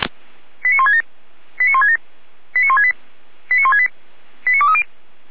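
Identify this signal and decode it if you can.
Five-tone selective call (ZVEI), heard as quick runs of five tones, tones of about 80 ms. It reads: 89167 89167 89168 89168 89230
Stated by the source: ZVEI five-tone call 89140 received by SDR, from the Feuerwehr Orlamünde audio archive.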